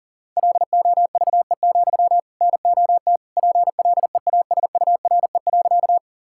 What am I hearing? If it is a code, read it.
Morse code, heard as a keyed tone, about 40 wpm, 700 Hz. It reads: LOVE, NOT PLEASURE.
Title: Sentences from Top 1600 Words 40wpm